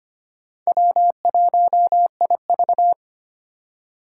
Morse code, heard as a keyed tone, about 25 wpm, 700 Hz. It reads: W1IV
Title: Four Challenging Letter-Number Combinations 25wpm